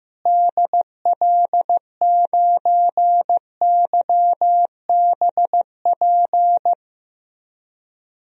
Morse code, heard as a keyed tone, about 16 wpm, 700 Hz. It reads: DL9YBP
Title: DXCC-100 Call Signs - 6 Characters - Repeated 2x - 15wpm